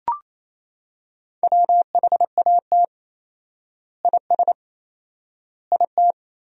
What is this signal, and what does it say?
Morse code, heard as a keyed tone, about 28 wpm, 700 Hz. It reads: WHAT IS IT
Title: 4x Word-Spacing - Sentences from Top 100 Words - 28wpm